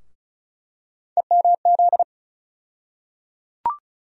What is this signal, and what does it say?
Morse code, heard as a keyed tone, about 36 wpm, 700 Hz. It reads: EMZ
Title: USA Call Sign Postfixes - 3 Letters 35wpm